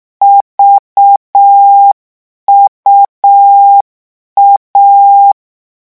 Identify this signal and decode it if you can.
Morse code, heard as a keyed tone, about 6 wpm, 800 Hz. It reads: VUA
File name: morse mp3 / lecon3-vitesse 6